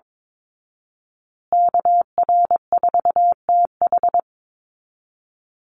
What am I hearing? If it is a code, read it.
Morse code, heard as a keyed tone, about 22 wpm, 690 Hz. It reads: KR4TH
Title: Call Signs Repeated 3x - 22wpm